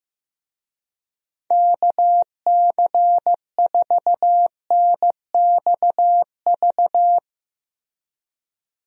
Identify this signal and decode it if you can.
Morse code, heard as a keyed tone, about 15 wpm, 700 Hz. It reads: KC4NXV
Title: USA Call Signs - 6 Characters - Repeated 3x - 15wpm